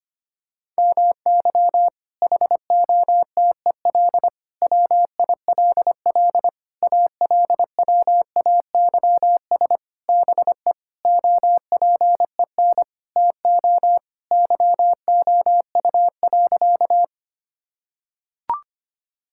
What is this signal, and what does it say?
Morse code, heard as a keyed tone, about 25 wpm, 700 Hz. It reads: MY HOTEL WILL ALWAYS BE OPEN TO YOU.
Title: Sentences from Top 1600 Words 25wpm